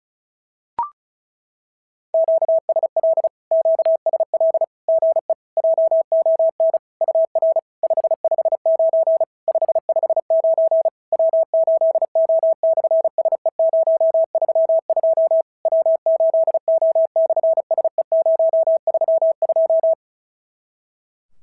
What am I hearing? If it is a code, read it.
Morse code, heard as a keyed tone, about 36 wpm, 650 Hz. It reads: QSL QSL GE JON UR 559 559 W8O/SE032 W8O/SE032
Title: SOTA QSO Practice 35wpm